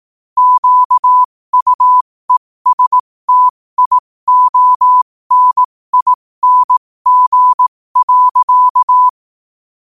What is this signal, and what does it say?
Morse code, heard as a keyed tone, about 18 wpm, spaced as for 15 wpm, 1000 Hz. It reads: QUESTIONING.
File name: qotdfiles / 15wpm